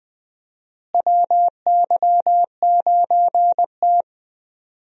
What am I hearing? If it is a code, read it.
Morse code, heard as a keyed tone, about 21 wpm, 700 Hz. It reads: WY9T
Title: USA Call Signs - 4 Characters - Repeated 3x - 20wpm